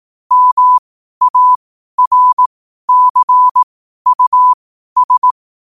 Morse code, heard as a keyed tone, about 18 wpm, spaced as for 12 wpm, 1000 Hz. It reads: MARCUS
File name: qotdfiles / 12wpm